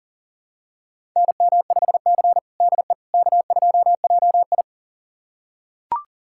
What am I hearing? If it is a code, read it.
Morse code, heard as a keyed tone, about 40 wpm, 700 Hz. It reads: NM5C DE K2JI
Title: USA Callsign DE Callsign - 4 Characters - 40wpm